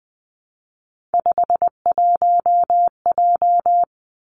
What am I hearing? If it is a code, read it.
Morse code, heard as a keyed tone, about 20 wpm, 700 Hz. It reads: 51J
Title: Three Challenging Letter-Number Combinations 20wpm